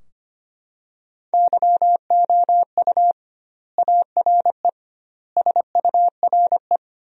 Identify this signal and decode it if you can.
Morse code, heard as a keyed tone, about 25 wpm, 700 Hz. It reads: YOU ARE SURE
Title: 2x Word-Spacing - Sentences from Top 300 Words - 25wpm